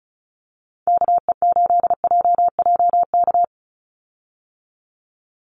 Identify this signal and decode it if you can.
Morse code, heard as a keyed tone, about 34 wpm, 700 Hz. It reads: KE8JJK
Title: USA Call Signs 35wpm